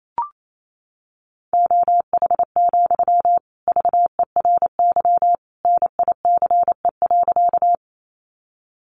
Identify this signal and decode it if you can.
Morse code, heard as a keyed tone, about 28 wpm, 700 Hz. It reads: OH, VERY NICE.
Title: Sentences from Top 1100 Words 28wpm